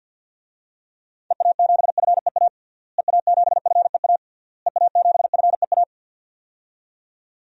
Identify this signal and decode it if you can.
Morse code, heard as a keyed tone, about 51 wpm, 700 Hz. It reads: EA7FEA EA7FEA EA7FEA
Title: DXCC-100 Call Signs - 6 Characters - Repeated 3x - 50wpm